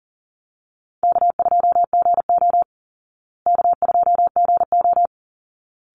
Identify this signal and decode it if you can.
Morse code, heard as a keyed tone, about 40 wpm, 700 Hz. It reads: K2GO K2GO
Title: Easy Call Signs Repeated 2x - 40wpm